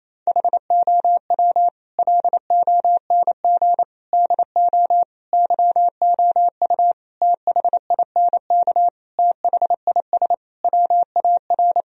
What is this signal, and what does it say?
Morse code, heard as a keyed tone, about 28 wpm, 700 Hz. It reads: HOW LONG DO YOU THINK THIS WAR
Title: Sentences from Top 1000 Words - Review Part 2 - 28wpm